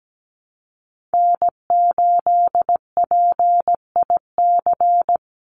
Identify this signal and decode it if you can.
Morse code, heard as a keyed tone, about 17 wpm, 700 Hz. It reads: N8PIC
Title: USA Call Signs 17wpm